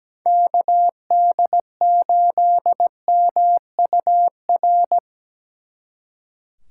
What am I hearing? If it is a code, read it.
Morse code, heard as a keyed tone, about 20 wpm, 700 Hz. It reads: KD8MUR
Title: Speed Racing - USA Call Signs 17wpm